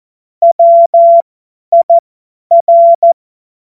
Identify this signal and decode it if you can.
Morse code, heard as a keyed tone, about 14 wpm, spaced as for 10 wpm, 680 Hz. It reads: WIR
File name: Morsekurs / Witze / Witz 20. Juni 2023 50 F70